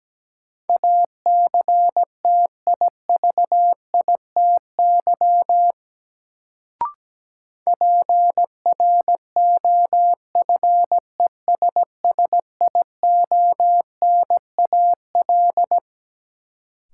Morse code, heard as a keyed tone, about 17 wpm, 700 Hz. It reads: ACTIVITY PROFESSIONAL
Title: Sentences from Top 1000 Words - Review Part 4 - New Word Drill - 17wpm